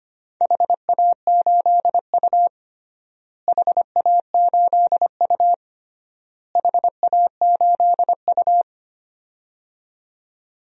Morse code, heard as a keyed tone, about 25 wpm, 700 Hz. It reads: HA8U HA8U HA8U
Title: DXCC-100 Call Signs - 4 Characters - Repeated 3x - 25wpm